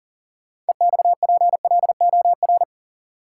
Morse code, heard as a keyed tone, about 40 wpm, 700 Hz. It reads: EXPLOR
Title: Sentences from Top 1100 Words - New Word Drill Alt - 40wpm